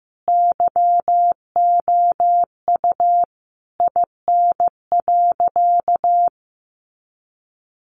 Morse code, heard as a keyed tone, about 14 wpm, 700 Hz. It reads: YOU IN.
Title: Sentences from Top 1500 Words 15wpm